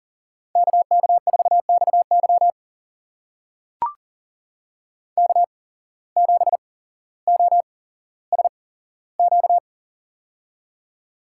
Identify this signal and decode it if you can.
Morse code, heard as a keyed tone, about 40 wpm, spaced as for 17 wpm, 700 Hz. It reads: KK4XY K7OSQ
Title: USA Call Signs - 5 Characters - 8x Character Spacing - 40wpm